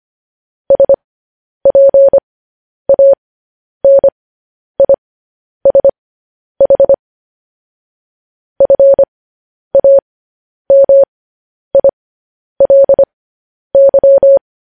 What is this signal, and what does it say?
Morse code, heard as a keyed tone, about 25 wpm, spaced as for 10 wpm, 550 Hz. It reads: SPANISH FAMILY
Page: news headlines in morse code at 10 wpm